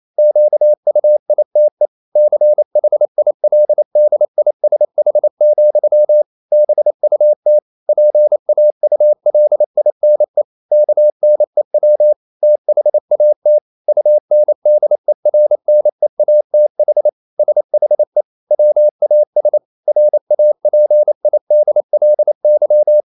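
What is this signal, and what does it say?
Morse code, heard as a keyed tone, about 28 wpm, 600 Hz. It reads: QUITE CHILDISH, BUT PAULINE KNEW THAT UNDERNEATH SHE WAS RAPIDLY
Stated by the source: Morse code practice at 28 wpm, text of Ken Follett's novel Never in English, from CW med Gnister.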